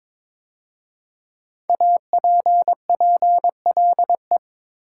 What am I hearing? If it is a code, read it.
Morse code, heard as a keyed tone, about 22 wpm, 700 Hz. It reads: APPLE
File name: 5 Letter Words - First 3 Letters Given - 22wpm